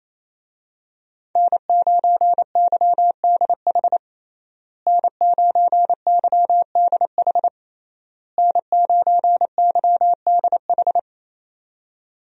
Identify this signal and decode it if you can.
Morse code, heard as a keyed tone, about 27 wpm, 700 Hz. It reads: N9YDH N9YDH N9YDH
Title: USA Call Signs - 5 Characters - Repeated 3x - 28wpm